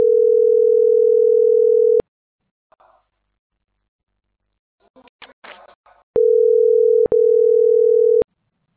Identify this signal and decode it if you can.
Ringback tone (North America), 440 + 480 Hz.